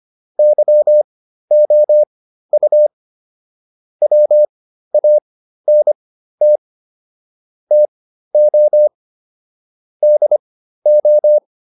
Morse code, heard as a keyed tone, about 25 wpm, spaced as for 13 wpm, 600 Hz. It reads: YOU WANT TO DO
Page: /intermediate-practice-files/